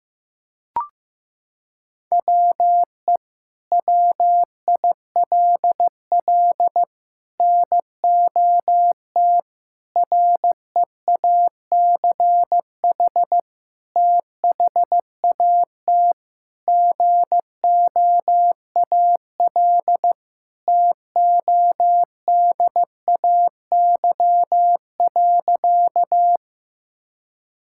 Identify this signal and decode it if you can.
Morse code, heard as a keyed tone, about 15 wpm, 700 Hz. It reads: WE WILL NOT REACH THAT GOAL TODAY.
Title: Sentences from Top 1000 Words - Review Part 5 - 15wpm